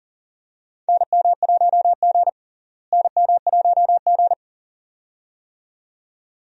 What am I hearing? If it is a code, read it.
Morse code, heard as a keyed tone, about 40 wpm, 700 Hz. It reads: NM1G NM1G
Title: USA Call Signs - 4 Characters - Repeated 2x - 40wpm